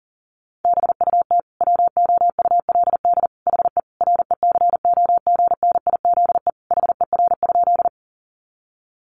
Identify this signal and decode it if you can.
Morse code, heard as a keyed tone, about 40 wpm, 700 Hz. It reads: BUT WOULD HE RECOGNIZE HER?
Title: Sentences from Top 1400 Words 40wpm